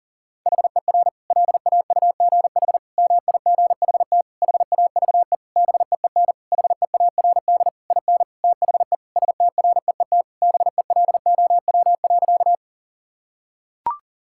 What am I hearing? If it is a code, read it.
Morse code, heard as a keyed tone, about 40 wpm, 700 Hz. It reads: HER LAUGH MIGHT HAVE BEEN HEARD IN THE STREET BELOW.
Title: Sentences from Top 1000 Words - Review Part 5 - 40wpm